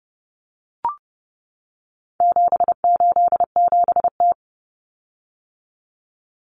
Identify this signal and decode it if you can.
Morse code, heard as a keyed tone, about 30 wpm, 700 Hz. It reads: 787T
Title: Common Cut Numbers - 4-Digit 30wpm